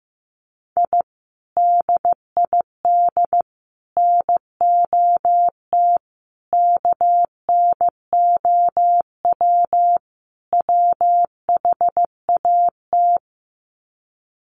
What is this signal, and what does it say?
Morse code, heard as a keyed tone, about 15 wpm, 700 Hz. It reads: I DID NOT KNOW WHAT
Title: Sets of 5 Words 15wpm